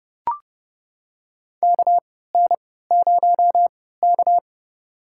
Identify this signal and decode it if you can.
Morse code, heard as a keyed tone, about 31 wpm, spaced as for 17 wpm, 700 Hz. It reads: KN0K
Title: USA Call Signs - 4 Characters - 3x Character Spacing - 30wpm